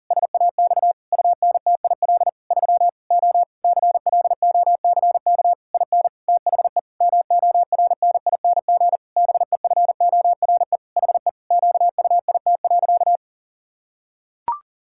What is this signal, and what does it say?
Morse code, heard as a keyed tone, about 40 wpm, 700 Hz. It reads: SAX UNTIL 3 O CLOCK IN THE MORNING BEFORE HE QUIT.
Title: Pangram Challenge v2 - 40wpm